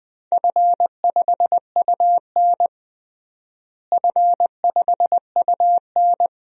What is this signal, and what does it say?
Morse code, heard as a keyed tone, about 20 wpm, 700 Hz. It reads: F5UN F5UN
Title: DXCC-100 Call Signs - 4 Characters - Repeated 2x - 20wpm